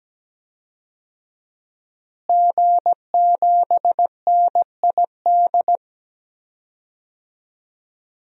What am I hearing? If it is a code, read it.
Morse code, heard as a keyed tone, about 17 wpm, 700 Hz. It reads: G7NID
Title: DXCC-100 Call Signs - 5 Characters - Repeated 3x - 17wpm